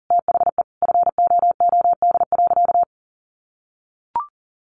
Morse code, heard as a keyed tone, about 40 wpm, 700 Hz. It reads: THE FOOD.
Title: Sentences from Top 1200 Words 40wpm